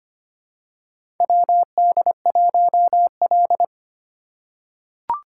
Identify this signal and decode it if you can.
Morse code, heard as a keyed tone, about 25 wpm, 700 Hz. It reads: WD1L
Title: USA Call Signs - 4 Characters - 2x Character Spacing - 25wpm